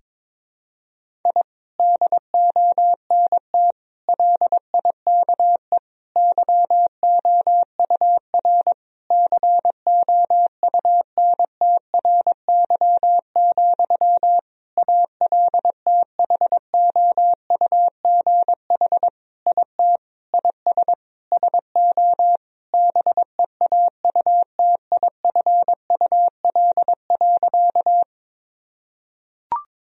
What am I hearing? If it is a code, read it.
Morse code, heard as a keyed tone, about 22 wpm, 700 Hz. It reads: I DONT LIKE YOUR COUNTRY, ALTHOUGH IT IS SO BEAUTIFUL.
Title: Sentences from Top 1000 Words - Review Part 3 - 22wpm